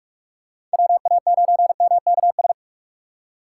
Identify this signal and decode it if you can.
Morse code, heard as a keyed tone, about 48 wpm, 700 Hz. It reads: WA9MKS